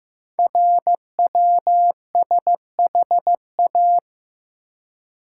Morse code, heard as a keyed tone, about 15 wpm, 690 Hz. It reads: RWSHA
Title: Code Groups - 5 Letters - 15wpm